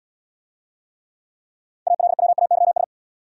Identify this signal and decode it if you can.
Morse code, heard as a keyed tone, about 71 wpm, 700 Hz. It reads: ALWAYS